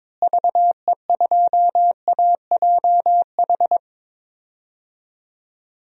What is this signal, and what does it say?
Morse code, heard as a keyed tone, about 22 wpm, 700 Hz. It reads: VE2AJH
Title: DXCC-100 Call Signs - 6 Characters - Repeated 2x - 22wpm